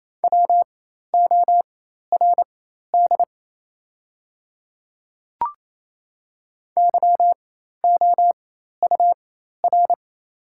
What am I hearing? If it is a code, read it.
Morse code, heard as a keyed tone, about 28 wpm, spaced as for 13 wpm, 690 Hz. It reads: WORD YOUR
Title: 4x Character Spacing - Top 100 Words - 28wpm